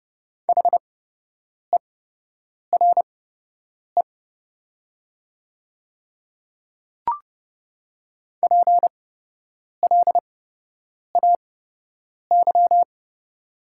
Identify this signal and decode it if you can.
Morse code, heard as a keyed tone, about 30 wpm, spaced as for 8 wpm, 700 Hz. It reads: HERE PLAY
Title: 8x Character Spacing - Top 200 Words - 30wpm